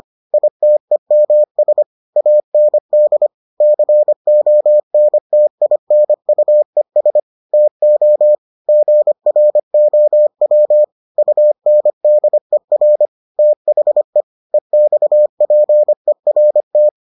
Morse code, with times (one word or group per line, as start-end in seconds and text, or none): 0.3-1.8 ITEMS
2.2-3.3 AND
3.6-7.2 CONTINUES
7.5-8.3 TO
8.7-10.8 GROW
11.2-13.1 UNDER
13.4-14.2 THE
14.5-16.9 EXPERT